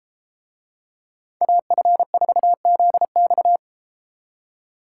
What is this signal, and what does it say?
Morse code, heard as a keyed tone, about 32 wpm, 700 Hz. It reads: AF4ZX